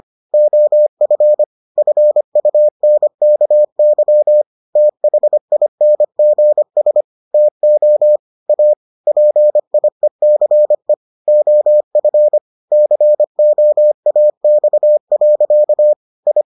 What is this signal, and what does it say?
Morse code, heard as a keyed tone, about 25 wpm, 600 Hz. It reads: OF FUNKY THINGS TO A PIECE OF COAX. I